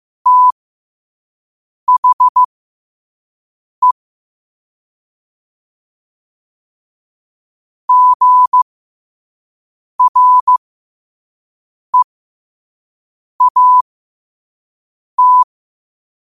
Morse code, heard as a keyed tone, about 15 wpm, spaced as for 5 wpm, 1000 Hz. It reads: THE GREAT
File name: qotdfiles / 5wpm